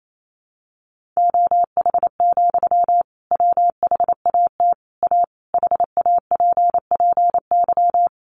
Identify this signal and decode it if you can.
Morse code, heard as a keyed tone, about 28 wpm, 700 Hz. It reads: OH, WHAT A HAPPY
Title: Sentences from Top 1500 Words 28wpm